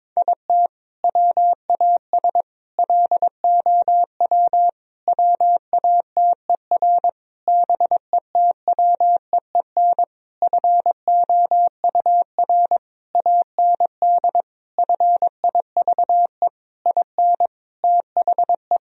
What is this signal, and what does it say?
Morse code, heard as a keyed tone, about 22 wpm, 700 Hz. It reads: IT WAS LOW WATER BETWEEN FOUR AND FIVE IN THE
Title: Sentences from Top 1000 Words - Review Part 2 - 22wpm